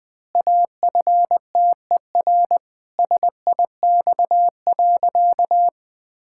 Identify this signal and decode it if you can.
Morse code, heard as a keyed tone, about 20 wpm, 700 Hz. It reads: AFTER SIX.
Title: Pangram Challenge v2 - 20wpm